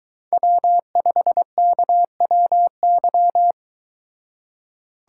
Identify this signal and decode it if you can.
Morse code, heard as a keyed tone, about 22 wpm, 700 Hz. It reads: W5KWY